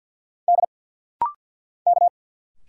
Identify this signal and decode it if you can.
Morse code, heard as a keyed tone, about 52 wpm, 700 Hz. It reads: D K
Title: Single Letters - Rapid-Fire 50wpm